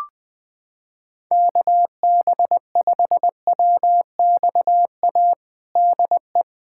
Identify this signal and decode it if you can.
Morse code, heard as a keyed tone, about 21 wpm, 700 Hz. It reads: KB5WXA DE
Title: USA Callsign DE Callsign - 6 Characters - 20wpm